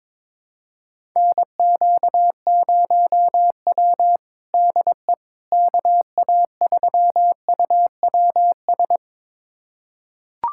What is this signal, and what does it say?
Morse code, heard as a keyed tone, about 22 wpm, 700 Hz. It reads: NQ0W DE KA3UWS
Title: USA Callsign DE Callsign 22wpm